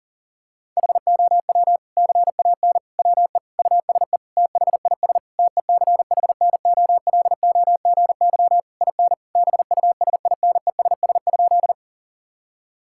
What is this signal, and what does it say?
Morse code, heard as a keyed tone, about 37 wpm, 700 Hz. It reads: HOW CAN WE USE THIS TECHNOLOGY IN BUSINESS?
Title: Sentences from Top 1000 Words - Review Part 5 - 40wpm